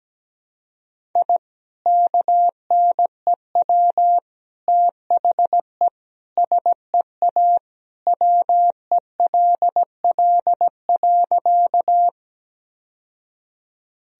Morse code, heard as a keyed tone, about 17 wpm, 700 Hz. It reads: I KNEW THE SEA WELL.
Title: Sentences from Top 1000 Words - Review Part 1 - 17wpm